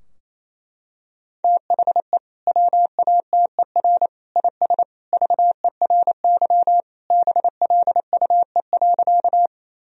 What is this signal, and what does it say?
Morse code, heard as a keyed tone, about 27 wpm, 700 Hz. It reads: THE WATER IS VERY BLUE.
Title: Sentences from Top 1000 Words - Review Part 3 - 28wpm